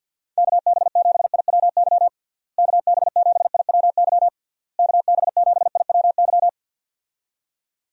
Morse code, heard as a keyed tone, about 52 wpm, 700 Hz. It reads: KB7IWY KB7IWY KB7IWY
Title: USA Call Signs - 6 Characters - Repeated 3x - 50wpm